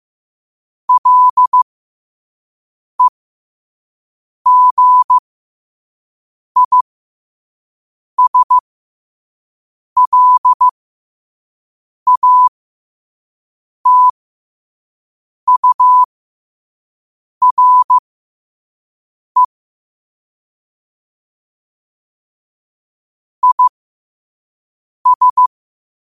Morse code, heard as a keyed tone, about 15 wpm, spaced as for 5 wpm, 1000 Hz. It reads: LEGISLATURE IS
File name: 5wpm